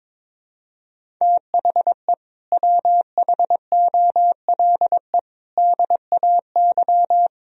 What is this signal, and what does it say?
Morse code, heard as a keyed tone, about 22 wpm, 700 Hz. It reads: THE WHOLE DAY
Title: Sentences from Top 500 Words and Abbreviations 22wpm